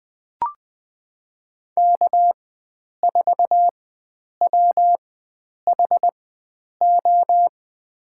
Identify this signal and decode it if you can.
Morse code, heard as a keyed tone, about 20 wpm, spaced as for 9 wpm, 700 Hz. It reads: K4WHO
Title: USA Call Signs - 4x Character Spacing - 20wpm